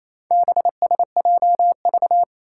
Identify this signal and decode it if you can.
Morse code, heard as a keyed tone, about 28 wpm, 700 Hz. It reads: BSJV
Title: Four Challenging Letter-Number Combinations 28wpm